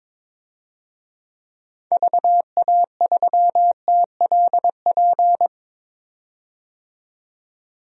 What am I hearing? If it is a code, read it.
Morse code, heard as a keyed tone, about 21 wpm, 700 Hz. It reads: VA3TLP
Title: DXCC-100 Call Signs - 6 Characters - 22wpm